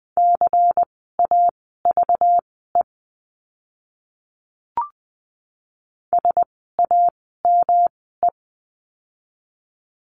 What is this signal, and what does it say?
Morse code, heard as a keyed tone, about 20 wpm, spaced as for 14 wpm, 700 Hz. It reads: CAVE SAME
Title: 192 Lesson 31f - Letter v - Words 20wpm